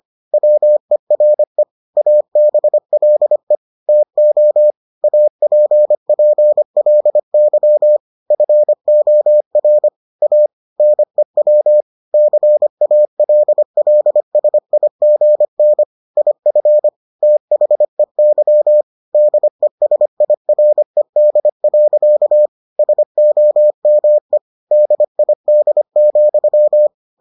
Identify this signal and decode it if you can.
Morse code, heard as a keyed tone, about 25 wpm, 600 Hz. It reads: WERE ABLE TO APPLY FOR A NEW CALLSIGN IF THEY DESIRED. SOME DID,